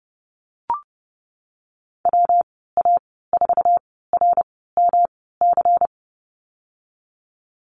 Morse code, heard as a keyed tone, about 30 wpm, spaced as for 17 wpm, 700 Hz. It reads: WA4RMC
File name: Lesson 37g - Letter x - Call Signs 30wpm